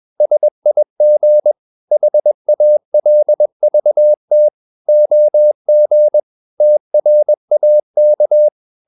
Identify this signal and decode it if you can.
Morse code, heard as a keyed tone, about 21 wpm, 600 Hz. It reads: SIG HALVT OG TRAK
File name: Ferie QRS 21wpm